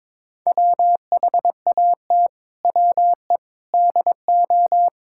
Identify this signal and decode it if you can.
Morse code, heard as a keyed tone, about 22 wpm, 700 Hz. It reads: WHAT WE DO
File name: practice / 22wpm / Sets of 7 Words - 22wpm